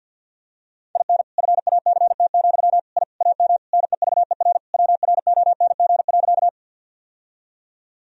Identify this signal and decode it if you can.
Morse code, heard as a keyed tone, about 52 wpm, 700 Hz. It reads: IN FACT, I AM NEVER WRONG.
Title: Sentences from Top 1000 Words - Review Part 3 - 50wpm